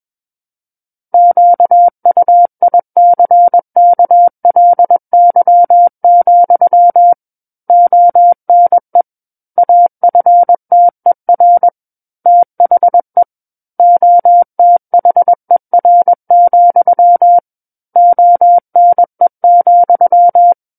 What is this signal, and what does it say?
Morse code, heard as a keyed tone, about 21 wpm, 700 Hz. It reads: QUICKLY, ONE AFTER THE OTHER, ONE,
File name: Book One / 12-Chapter 12 - 21 WPM